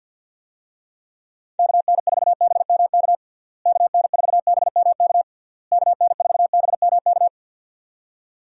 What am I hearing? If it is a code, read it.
Morse code, heard as a keyed tone, about 51 wpm, 700 Hz. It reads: KN4BMK KN4BMK KN4BMK
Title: USA Call Signs - 6 Characters - Repeated 3x - 50wpm